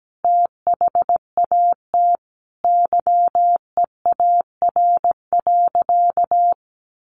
Morse code, heard as a keyed tone, about 17 wpm, 700 Hz. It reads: THAT YEAR.
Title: Sentences from Top 1200 Words 17wpm